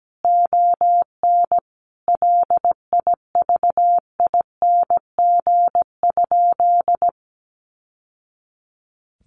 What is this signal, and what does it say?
Morse code, heard as a keyed tone, about 17 wpm, 700 Hz. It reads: ON LIVING?
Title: Sentences from Top 1100 Words 17wpm